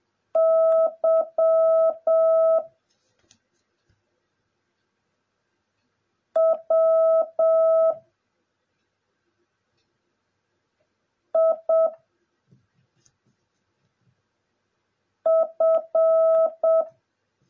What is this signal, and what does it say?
Morse code, heard as a keyed tone, about 7 wpm, 650 Hz. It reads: Y W I F
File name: jh8cbh / 2021cw / cw09a